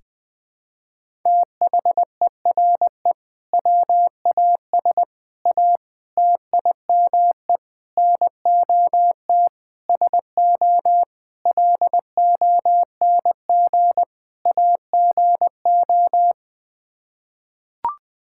Morse code, heard as a keyed tone, about 20 wpm, 700 Hz. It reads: THERE WAS A TIME NOT SO LONG AGO
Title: Sets of 8 Words - 20wpm